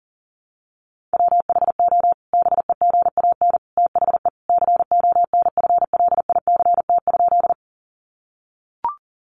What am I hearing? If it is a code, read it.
Morse code, heard as a keyed tone, about 40 wpm, 700 Hz. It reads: WHO BEGAN THE CONFLICT?
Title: Sentences from Top 1300 Words 40wpm